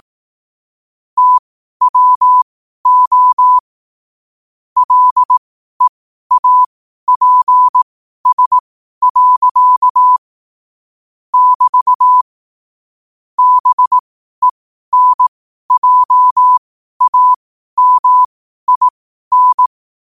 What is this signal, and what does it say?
Morse code, heard as a keyed tone, about 18 wpm, spaced as for 12 wpm, 1000 Hz. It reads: TWO LEAPS. <BT> BENJAMIN
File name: qotdfiles / 12wpm